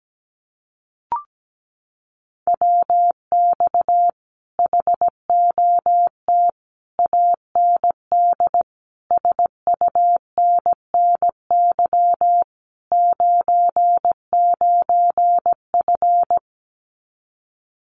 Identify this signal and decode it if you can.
Morse code, heard as a keyed tone, about 17 wpm, 700 Hz. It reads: WX HOT AND SUNNY 99F
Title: Common QSO Phrases Encore 17wpm